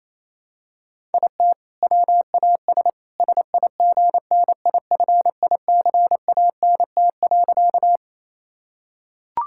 Morse code, heard as a keyed tone, about 28 wpm, 700 Hz. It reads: IT WAS SIGNIFICANT.
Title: Sentences from Top 1000 Words - Review Part 6 - 28wpm